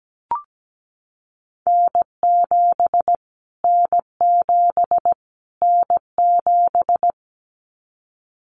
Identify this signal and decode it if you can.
Morse code, heard as a keyed tone, about 17 wpm, 700 Hz. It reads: N7 N7 N7
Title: Callsign Prefixes for Top 25 DXCC - Repeated 3x - 17wpm